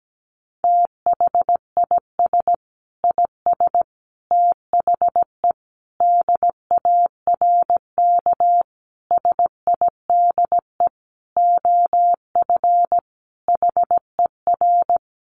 Morse code, heard as a keyed tone, about 17 wpm, 700 Hz. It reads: THIS IS THE DARK SIDE OF HER
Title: Sentences from Top 1200 Words 17wpm